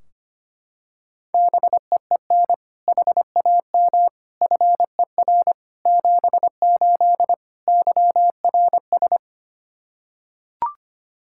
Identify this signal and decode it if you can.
Morse code, heard as a keyed tone, about 25 wpm, 700 Hz. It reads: BEEN HAM FER 78 YRS
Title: Practice QSO - Exchange 3 - More Info - 25wpm